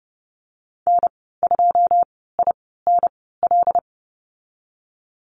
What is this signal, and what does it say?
Morse code, heard as a keyed tone, about 30 wpm, spaced as for 17 wpm, 700 Hz. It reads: N2INL
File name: Lesson 15g - Number 5 - Call Signs 30wpm